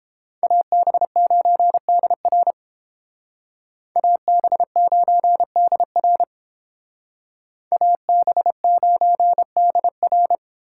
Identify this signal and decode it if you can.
Morse code, heard as a keyed tone, about 30 wpm, 700 Hz. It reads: AB9DR AB9DR AB9DR